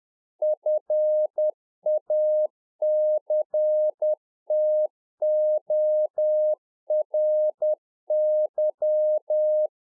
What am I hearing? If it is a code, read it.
Morse code, heard as a keyed tone, about 10 wpm, 610 Hz. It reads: FACTORY
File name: Czech Army Morse Keys 10WPM